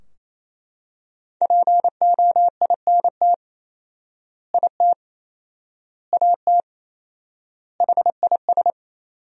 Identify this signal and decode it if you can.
Morse code, heard as a keyed tone, about 28 wpm, 700 Hz. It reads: POINT IT AT HIS
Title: 4x Word-Spacing - Sentences from Top 300 Words - 28wpm